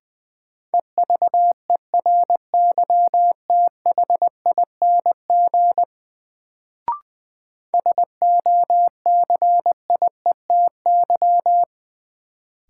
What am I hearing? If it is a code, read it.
Morse code, heard as a keyed tone, about 21 wpm, 700 Hz. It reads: EVERYTHING SOCIETY
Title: Sentences from Top 700 Words - New Word Drill - 20wpm